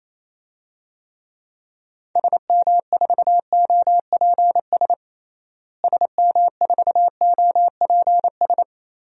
Morse code, heard as a keyed tone, about 28 wpm, 700 Hz. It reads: SM4OPS SM4OPS
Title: DXCC-100 Call Signs - 6 Characters - Repeated 3x - 28wpm